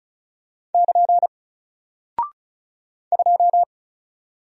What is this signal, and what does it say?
Morse code, heard as a keyed tone, about 36 wpm, 700 Hz. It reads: <KN> 2
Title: Single Character (Letter:Number:Punctuation:Prosign) 35wpm